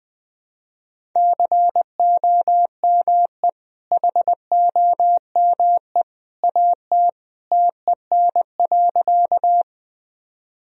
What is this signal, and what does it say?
Morse code, heard as a keyed tone, about 20 wpm, 700 Hz. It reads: COME HOME AT TEN.
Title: Sentences from Top 1000 Words - Review Part 2 - 20wpm